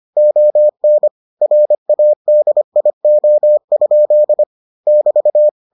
Morse code, heard as a keyed tone, about 25 wpm, 600 Hz. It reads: ON RADIO? <BT>